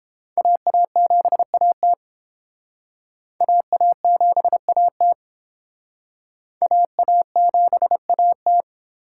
Morse code, heard as a keyed tone, about 29 wpm, 700 Hz. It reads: AA7AT AA7AT AA7AT